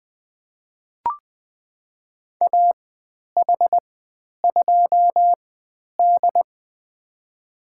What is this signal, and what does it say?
Morse code, heard as a keyed tone, about 20 wpm, spaced as for 10 wpm, 700 Hz. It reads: AH2D
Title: 095 Lesson 16g - Letter c - Call Signs 20wpm